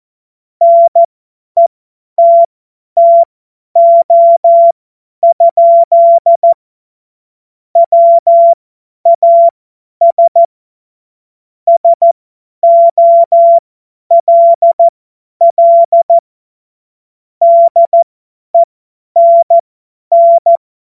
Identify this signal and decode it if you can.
Morse code, heard as a keyed tone, about 14 wpm, spaced as for 10 wpm, 680 Hz. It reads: NETTO? WAS SOLL DENN
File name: Morsekurs / Witze / Witz 19. April 2022 50 F70